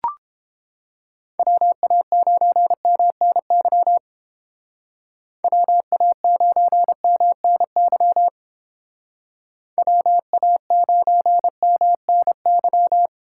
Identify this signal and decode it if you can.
Morse code, heard as a keyed tone, about 31 wpm, 700 Hz. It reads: WA9MNY WA9MNY WA9MNY